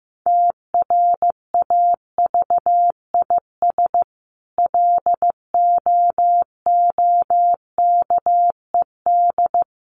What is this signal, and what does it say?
Morse code, heard as a keyed tone, about 15 wpm, 700 Hz. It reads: TRAVIS LOOKED
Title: A Sound Of Thunder 15wpm